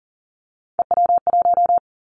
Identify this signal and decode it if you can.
Morse code, heard as a keyed tone, about 40 wpm, 700 Hz. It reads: EW1
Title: Callsign Prefixes for Top 100 DXCC - 40wpm